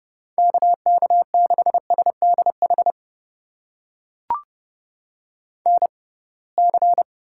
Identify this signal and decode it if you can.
Morse code, heard as a keyed tone, about 30 wpm, spaced as for 23 wpm, 700 Hz. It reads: KK6SDH NC
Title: USA Call Signs - 6x Character Spacing - 30wpm